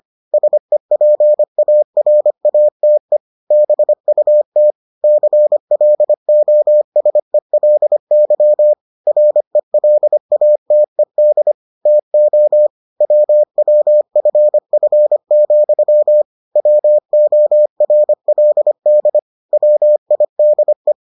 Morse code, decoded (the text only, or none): SEPARATE BUT CLOSELY RELATED TO WWFF, WORLD WIDE